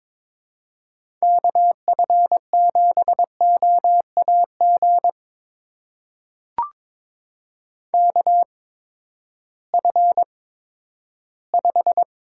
Morse code, heard as a keyed tone, about 22 wpm, spaced as for 13 wpm, 700 Hz. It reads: KF7OAG KF5
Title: USA Call Signs - 6 Characters - 8x Character Spacing - 22wpm